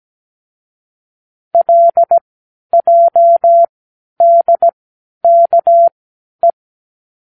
Morse code, heard as a keyed tone, about 17 wpm, spaced as for 11 wpm, 680 Hz. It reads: LJDKE
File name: k7qo / Lesson18(A-L)